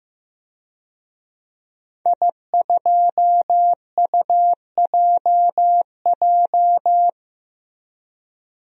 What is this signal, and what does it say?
Morse code, heard as a keyed tone, about 15 wpm, 700 Hz. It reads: I2UJJ
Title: DXCC-100 Call Signs - 5 Characters - 15wpm